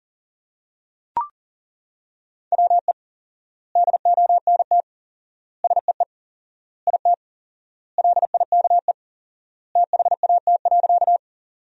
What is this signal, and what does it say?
Morse code, heard as a keyed tone, about 40 wpm, 700 Hz. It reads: WE DONT SEE IT LIKE THAT.
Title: 4x Word-Spacing - Sentences from Top 300 Words - 40wpm